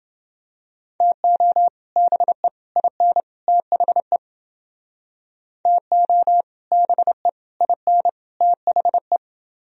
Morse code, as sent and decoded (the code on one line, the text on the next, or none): - --- / -... / .. -. / - .... / - --- / -... / .. -. / - .... .
TO BE IN THE TO BE IN THE